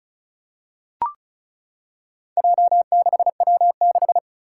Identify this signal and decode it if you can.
Morse code, heard as a keyed tone, about 36 wpm, 700 Hz. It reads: J6W6
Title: Four Challenging Letter-Number Combinations 35wpm